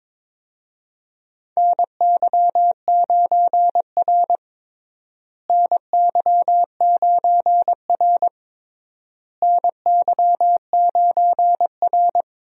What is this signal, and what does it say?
Morse code, heard as a keyed tone, about 22 wpm, 700 Hz. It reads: NY9R NY9R NY9R
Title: USA Call Signs - 4 Characters - Repeated 3x - 22wpm